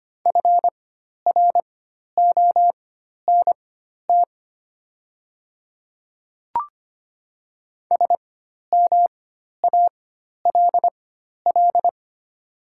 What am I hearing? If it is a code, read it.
Morse code, heard as a keyed tone, about 25 wpm, spaced as for 12 wpm, 690 Hz. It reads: FRONT SMALL
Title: 4x Character Spacing - Top 500 Words - 25wpm